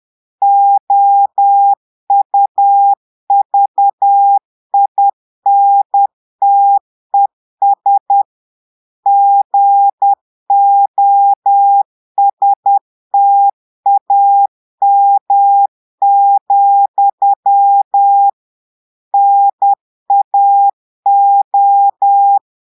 Morse code, heard as a keyed tone, about 10 wpm, 800 Hz. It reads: OUVINTES GOSTAM, NAO
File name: portugais texte en clair 2 vitesse 10 mots minutes